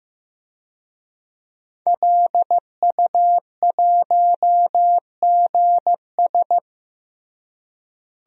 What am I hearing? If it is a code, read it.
Morse code, heard as a keyed tone, about 16 wpm, 700 Hz. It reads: LU1GS
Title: DXCC-100 Call Signs - 5 Characters - Repeated 3x - 15wpm